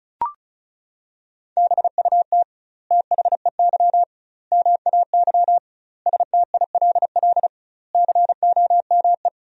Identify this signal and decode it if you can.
Morse code, heard as a keyed tone, about 34 wpm, 700 Hz. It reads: BUT THEY MAY STILL COME
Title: 2x Word-Spacing - Sentences from Top 300 Words - 35wpm